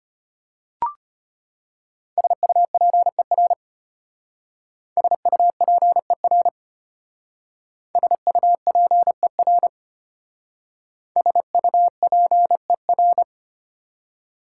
Morse code, heard as a keyed tone, about 31 wpm, 700 Hz. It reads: SUPER SUPER SUPER SUPER
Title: Speed-Racing Prefixes and Suffixes 25wpm